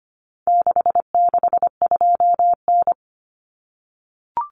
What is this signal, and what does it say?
Morse code, heard as a keyed tone, about 25 wpm, 700 Hz. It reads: 662N
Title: Common Cut Numbers - 4-Digit 25wpm